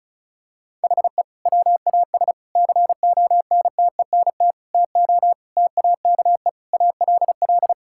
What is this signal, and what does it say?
Morse code, heard as a keyed tone, about 36 wpm, 700 Hz. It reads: HE WAS CONTENT TO TAKE ALL
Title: Sentences from Top 1000 Words - Review Part 6 - 35wpm